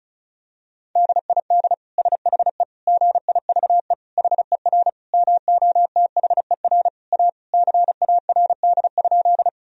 Morse code, heard as a keyed tone, about 36 wpm, 700 Hz. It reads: DID SHE GIVE HER MOTHER A CARD?
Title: Sentences from Top 1000 Words - Review Part 4 - 35wpm